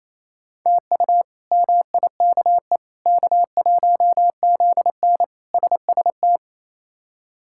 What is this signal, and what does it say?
Morse code, heard as a keyed tone, about 28 wpm, 700 Hz. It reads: TU MIKE K1ZN SST
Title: K1USN SST QSO Practice - 28wpm